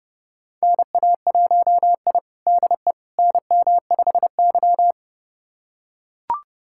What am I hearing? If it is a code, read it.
Morse code, heard as a keyed tone, about 30 wpm, 700 Hz. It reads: NA1I DE NM5Y
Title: USA Callsign DE Callsign - 4 Characters - 30wpm